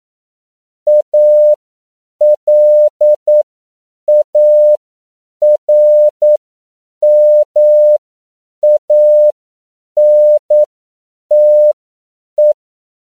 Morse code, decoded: ALARMANTE